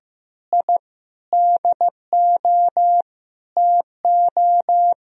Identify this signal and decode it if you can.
Morse code, heard as a keyed tone, about 15 wpm, 700 Hz. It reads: I DO TO
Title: Sentences from Top 1000 Words - Review Part 2 - 15wpm